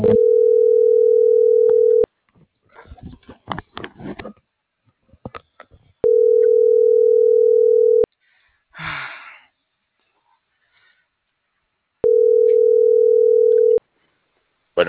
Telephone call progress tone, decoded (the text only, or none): ringback tone